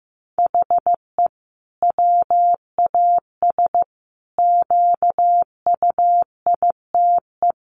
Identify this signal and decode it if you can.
Morse code, heard as a keyed tone, about 15 wpm, 700 Hz. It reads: HE WAS QUITE
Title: Sentences from Top 1000 Words 15wpm